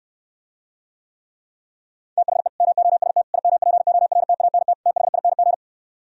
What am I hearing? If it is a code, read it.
Morse code, heard as a keyed tone, about 69 wpm, 700 Hz. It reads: THE MOST IMPORTANT THING